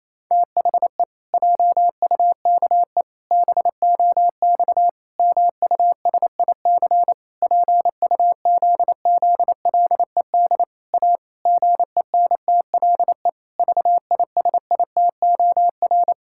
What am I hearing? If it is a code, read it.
Morse code, heard as a keyed tone, about 28 wpm, 700 Hz. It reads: THE JUKE BOX MUSIC PUZZLED A GENTLE VISITOR
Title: Pangram Challenge v2 - 28wpm